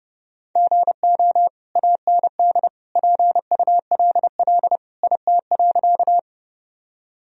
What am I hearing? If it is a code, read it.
Morse code, heard as a keyed tone, about 30 wpm, 700 Hz. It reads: GO AND PULL IT.
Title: Sentences from Top 1000 Words - Review Part 2 - 30wpm